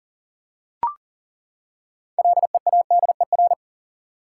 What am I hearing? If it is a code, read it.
Morse code, heard as a keyed tone, about 40 wpm, 700 Hz. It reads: LEADER
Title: Six Letter Words - 40wpm